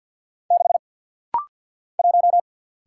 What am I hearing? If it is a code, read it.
Morse code, heard as a keyed tone, about 50 wpm, 700 Hz. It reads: 6 1
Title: Single Letter-Number - Rapid-Fire 50wpm